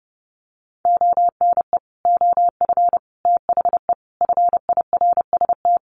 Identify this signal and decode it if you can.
Morse code, heard as a keyed tone, about 30 wpm, 700 Hz. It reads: ONE OF THE FIRST
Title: Sets of 5 Words Encore 30wpm